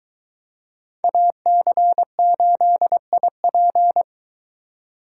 Morse code, heard as a keyed tone, about 23 wpm, 700 Hz. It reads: AC8IP